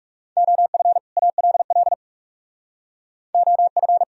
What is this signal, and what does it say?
Morse code, heard as a keyed tone, about 44 wpm, 700 Hz. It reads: OF ALL OF